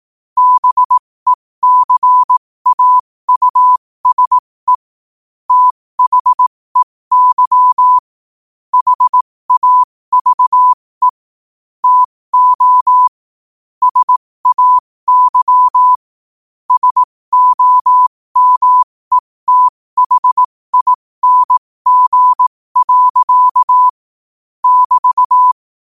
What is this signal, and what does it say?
Morse code, heard as a keyed tone, about 18 wpm, spaced as for 15 wpm, 1000 Hz. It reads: BECAUSE THEY HAVE TO SAY SOMETHING. <BT>
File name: qotdfiles / 15wpm